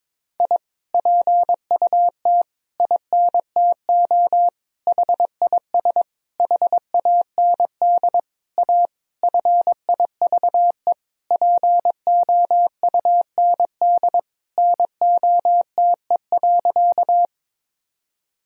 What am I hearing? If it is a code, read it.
Morse code, heard as a keyed tone, about 21 wpm, 700 Hz. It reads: I PUT INTO HIS HAND A FIVE POUND NOTE.
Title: Sentences from Top 1000 Words - Review Part 2 - 22wpm